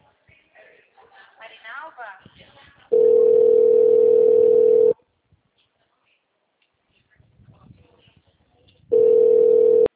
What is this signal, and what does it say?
Ringback tone (North America), 440 + 480 Hz, with speech.
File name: out- -2014- - - .55879